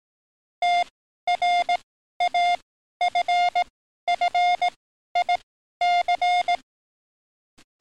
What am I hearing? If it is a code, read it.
Morse code, heard as a keyed tone, about 18 wpm, spaced as for 12 wpm, 710 Hz. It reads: TRAFFIC
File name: training / NAVPERS 11049to11060 / NAVPERS 11053 RA RB